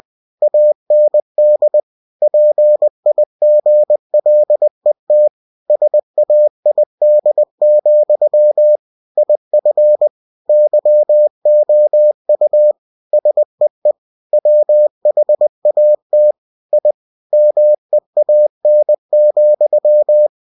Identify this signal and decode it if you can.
Morse code, heard as a keyed tone, about 20 wpm, 600 Hz. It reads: AND PIGLET SAID, IF YOU SEE WHAT I MEAN,
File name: cw-books / winnie-the-pooh / winnie-20wpm-page-0034